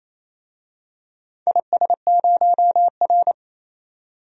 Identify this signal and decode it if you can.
Morse code, heard as a keyed tone, about 27 wpm, 700 Hz. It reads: IS0R
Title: DXCC-100 Call Signs - 4 Characters - 28wpm